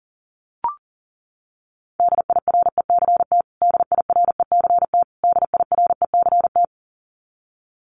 Morse code, heard as a keyed tone, about 40 wpm, 700 Hz. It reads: DIRECT DIRECT DIRECT
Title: Top 500 Words Repeated 40wpm